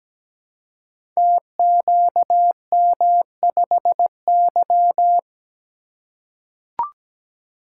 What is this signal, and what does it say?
Morse code, heard as a keyed tone, about 17 wpm, 700 Hz. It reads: TQM5Y
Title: Code Groups - 5 Letter:Numbers - 17wpm